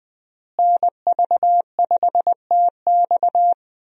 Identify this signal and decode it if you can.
Morse code, heard as a keyed tone, about 20 wpm, 700 Hz. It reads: NV5TX
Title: USA Call Signs - 5 Characters - Repeated 2x - 20wpm